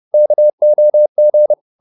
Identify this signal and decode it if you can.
Morse code, heard as a keyed tone, about 30 wpm, 600 Hz. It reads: KOG
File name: Korte ord DK 30wpm